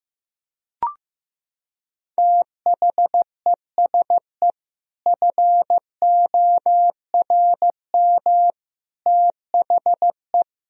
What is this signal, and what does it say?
Morse code, heard as a keyed tone, about 16 wpm, 700 Hz. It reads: THESE FORM THE
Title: Sentences from Top 1000 Words - Review Part 1 - 15wpm